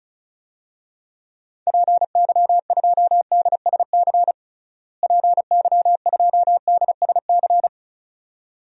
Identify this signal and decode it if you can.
Morse code, heard as a keyed tone, about 36 wpm, 700 Hz. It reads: PY2DSC PY2DSC
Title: DXCC-100 Call Signs - 6 Characters - Repeated 2x - 35wpm